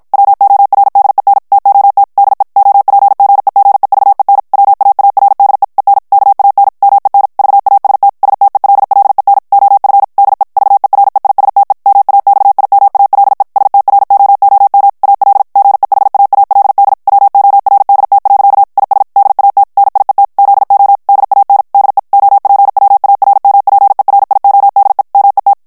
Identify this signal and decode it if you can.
Morse code, heard as a keyed tone, about 52 wpm, 800 Hz. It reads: WORDEN TOT DE OPGEGEVEN WAARDE EN KAN MEN VAST STELLEN OF DE VEREISTE MAXIMALE STROOM AL GEHAALD WORDT. IS DAT NIET ZO DAN DE OPWARMPERIODE MET
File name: HOLLANDAIS texte clair 2 vitesse 50 mots minute